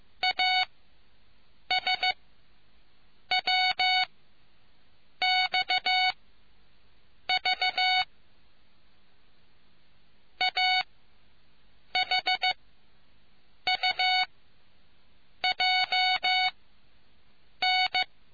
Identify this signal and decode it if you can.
Morse code, heard as a keyed tone, about 15 wpm, spaced as for 6 wpm, 740 Hz. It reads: ASWXV AHUJN